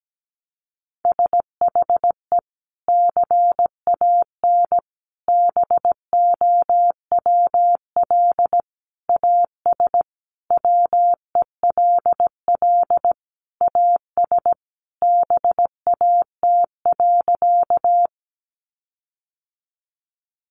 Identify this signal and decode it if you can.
Morse code, heard as a keyed tone, about 17 wpm, 700 Hz. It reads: SHE CAN BOWL AS WELL AS BAT.
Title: Sentences from Top 1500 Words 17wpm